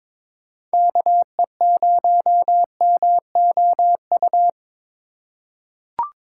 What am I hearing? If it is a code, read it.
Morse code, heard as a keyed tone, about 22 wpm, 700 Hz. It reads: KE0MOU
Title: USA Call Signs - 6 Characters - 2x Character Spacing - 22wpm